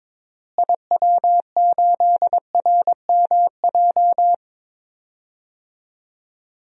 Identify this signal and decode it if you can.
Morse code, heard as a keyed tone, about 22 wpm, 700 Hz. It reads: IW8RMJ
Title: DXCC-100 Call Signs - 6 Characters - Repeated 2x - 22wpm